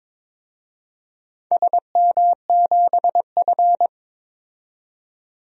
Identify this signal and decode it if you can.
Morse code, heard as a keyed tone, about 22 wpm, 700 Hz. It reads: SM7F
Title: DXCC-100 Call Signs - 4 Characters - 22wpm